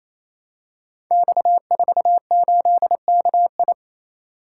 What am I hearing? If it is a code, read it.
Morse code, heard as a keyed tone, about 29 wpm, 700 Hz. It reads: X48KI
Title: Code Groups - 5 Letter:Numbers - 28wpm